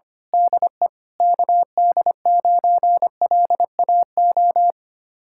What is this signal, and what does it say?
Morse code, heard as a keyed tone, about 25 wpm, 700 Hz. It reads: DE KD9LAO
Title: Speed-Racing Common QSO Phrases - 25wpm